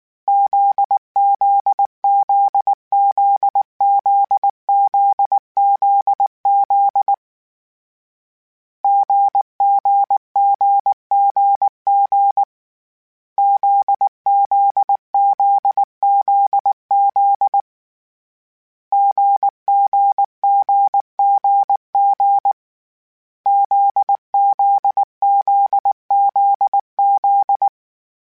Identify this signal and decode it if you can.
Morse code, heard as a keyed tone, about 19 wpm, 800 Hz. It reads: ZZZZZZZZ GGGGG ZZZZZ GGGGG ZZZZZ
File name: Difficult Combos / Difficult Combos-mp3s / GZ GZ7 @18